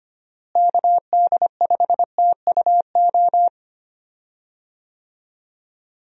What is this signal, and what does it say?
Morse code, heard as a keyed tone, about 25 wpm, 700 Hz. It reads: KD5TUO
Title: USA Call Signs - 6 Characters - Repeated 2x - 25wpm